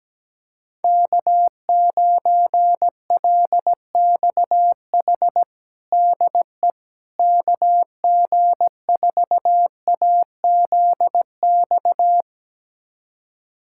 Morse code, heard as a keyed tone, about 17 wpm, 700 Hz. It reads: K9LXH DE KG4AZX
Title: USA Callsign DE Callsign 17wpm